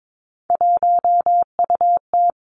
Morse code, heard as a keyed tone, about 22 wpm, 700 Hz. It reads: 1UT
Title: Three Challenging Letter-Number Combinations 22wpm